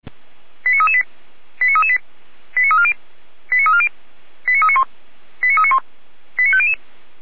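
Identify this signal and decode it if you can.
Five-tone selective call (ZVEI), heard as quick runs of five tones, tones of about 70 ms. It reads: 89338 89338 89340 89340 89391 89391 89500
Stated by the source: ZVEI five-tone selective call 89325 received on an SDR.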